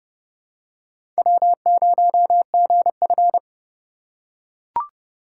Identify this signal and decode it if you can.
Morse code, heard as a keyed tone, about 30 wpm, 700 Hz. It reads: W0GF
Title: USA Call Signs - 4 Characters - 6x Character Spacing - 30wpm